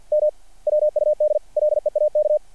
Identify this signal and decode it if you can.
Morse code, heard as a keyed tone, about 49 wpm, 600 Hz. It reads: M WUD LEAK
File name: archived / CRSnet